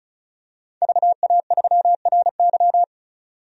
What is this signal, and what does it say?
Morse code, heard as a keyed tone, about 36 wpm, 700 Hz. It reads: VA3RY
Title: DXCC-100 Call Signs - 5 Characters - 35wpm